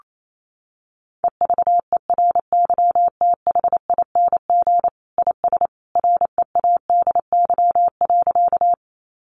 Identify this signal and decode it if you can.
Morse code, heard as a keyed tone, about 28 wpm, 700 Hz. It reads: EVERYTHING IS READY.
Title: Sentences from Top 700 Words 28wpm